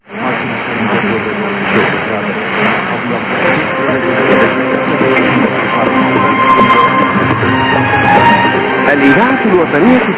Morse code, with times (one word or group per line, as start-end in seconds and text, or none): none